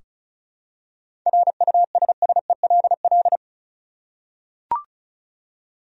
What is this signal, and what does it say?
Morse code, heard as a keyed tone, about 34 wpm, 700 Hz. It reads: RUSSELL
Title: USA Common Names - 35wpm